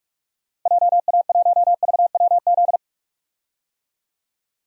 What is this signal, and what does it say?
Morse code, heard as a keyed tone, about 46 wpm, 700 Hz. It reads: JA1VWZ
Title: DXCC-100 Call Signs - 6 Characters - Repeated 3x - 45wpm